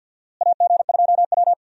Morse code, heard as a keyed tone, about 46 wpm, 700 Hz. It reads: AG2W
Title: USA Call Signs - 4 Characters - Repeated 3x - 50wpm